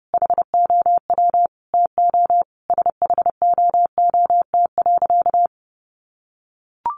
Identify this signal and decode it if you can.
Morse code, heard as a keyed tone, about 30 wpm, 700 Hz. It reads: HOW TO SHOOT.
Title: Sentences from Top 1200 Words 30wpm